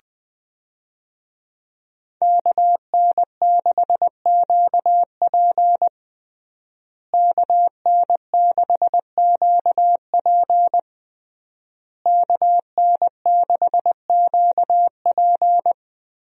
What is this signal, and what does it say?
Morse code, heard as a keyed tone, about 19 wpm, 700 Hz. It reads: KN6QP KN6QP KN6QP
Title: USA Call Signs - 5 Characters - Repeated 3x - 20wpm